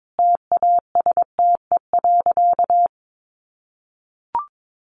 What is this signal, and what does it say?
Morse code, heard as a keyed tone, about 22 wpm, 700 Hz. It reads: TASTE.
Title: Sentences from Top 1100 Words 22wpm